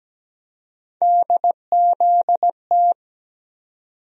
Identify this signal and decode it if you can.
Morse code, heard as a keyed tone, about 17 wpm, 700 Hz. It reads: DZT
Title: USA Call Sign Postfixes - 3 Letters 17wpm